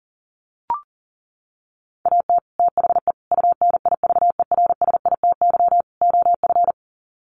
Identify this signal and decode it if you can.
Morse code, heard as a keyed tone, about 37 wpm, 700 Hz. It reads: AT THE UNIVERSITY OF
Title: Sets of 4 Words 40wpm